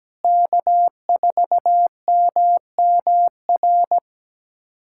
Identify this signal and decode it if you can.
Morse code, heard as a keyed tone, about 17 wpm, 700 Hz. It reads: K4MMR
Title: USA Callsign DE Callsign - 5 Characters - 17wpm